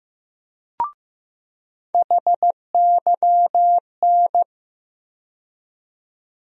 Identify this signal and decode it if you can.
Morse code, heard as a keyed tone, about 16 wpm, 700 Hz. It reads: HYN
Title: USA Call Sign Postfixes - 3 Letters 15wpm